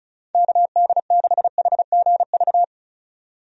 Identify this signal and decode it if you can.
Morse code, heard as a keyed tone, about 36 wpm, 700 Hz. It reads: KD6HGV
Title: USA Call Signs - 6 Characters - 35wpm